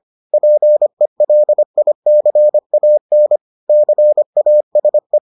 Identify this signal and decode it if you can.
Morse code, heard as a keyed tone, about 25 wpm, 600 Hz. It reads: PELICAN CASE